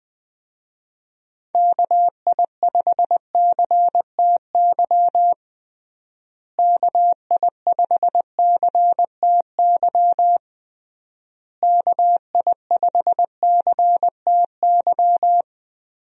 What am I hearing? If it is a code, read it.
Morse code, heard as a keyed tone, about 20 wpm, 700 Hz. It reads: KI5CTY KI5CTY KI5CTY
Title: USA Call Signs - 6 Characters - Repeated 3x - 20wpm